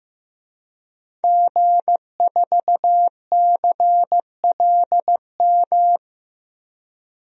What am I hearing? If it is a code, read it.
Morse code, heard as a keyed tone, about 15 wpm, 700 Hz. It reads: G4CLM
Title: DXCC-100 Call Signs - 5 Characters - 15wpm